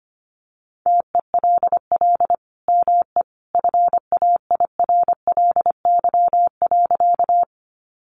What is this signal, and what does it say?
Morse code, heard as a keyed tone, about 25 wpm, 700 Hz. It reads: TELL ME FAIRLY.
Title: Sentences from Top 1100 Words 25wpm